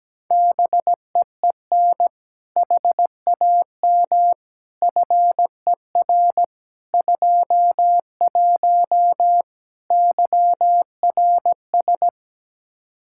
Morse code, heard as a keyed tone, about 17 wpm, 700 Hz. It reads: BEEN HAM FER 21 YRS
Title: Practice QSO - Exchange 3 - More Info - 17wpm